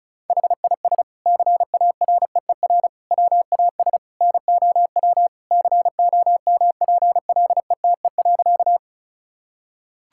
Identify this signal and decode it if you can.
Morse code, heard as a keyed tone, about 34 wpm, 700 Hz. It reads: HIS CAREER WAS NOW COMPLETE.
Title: Sentences from Top 1000 Words - Review Part 6 - 35wpm